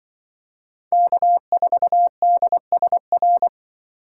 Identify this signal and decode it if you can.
Morse code, heard as a keyed tone, about 24 wpm, 700 Hz. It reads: K4DSR
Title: Speed Racing - USA 1x3 Call Signs 20wpm